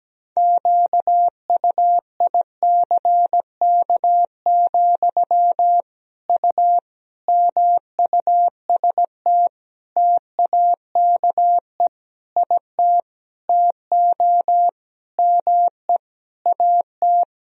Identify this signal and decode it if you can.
Morse code, heard as a keyed tone, about 17 wpm, 700 Hz. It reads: QUICK, U MUST TAKE IT TO ME AT
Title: Sentences from Top 500 Words and Abbreviations 17wpm